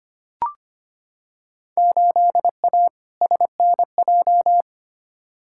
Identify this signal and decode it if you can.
Morse code, heard as a keyed tone, about 25 wpm, 700 Hz. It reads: 8A SNJ
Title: Field Day Exchanges Encore - 25wpm